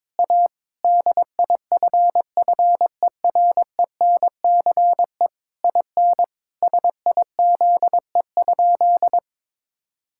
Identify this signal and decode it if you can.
Morse code, heard as a keyed tone, about 22 wpm, 700 Hz. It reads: A DIFFERENCE IN SIZE?
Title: Sentences from Top 1000 Words - Review Part 5 - 22wpm